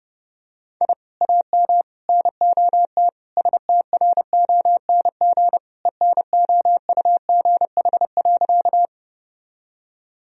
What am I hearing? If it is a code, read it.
Morse code, heard as a keyed tone, about 30 wpm, 700 Hz. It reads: I AM NOT STRONG ENOUGH.
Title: Sentences from Top 1000 Words - Review Part 3 - 30wpm